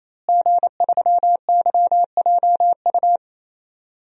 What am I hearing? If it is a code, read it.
Morse code, heard as a keyed tone, about 28 wpm, 700 Hz. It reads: G3YJU
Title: DXCC-100 Call Signs - 5 Characters - Repeated 3x - 28wpm